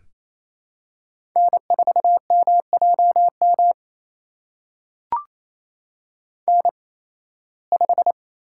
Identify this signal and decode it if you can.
Morse code, heard as a keyed tone, about 28 wpm, spaced as for 18 wpm, 700 Hz. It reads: N4MJM N5
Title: USA Call Signs - 8x Character Spacing - 28wpm